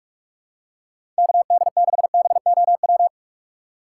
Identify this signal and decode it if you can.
Morse code, heard as a keyed tone, about 44 wpm, 700 Hz. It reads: KD6BOW
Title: USA Call Signs - 6 Characters - 2x Character Spacing - 45wpm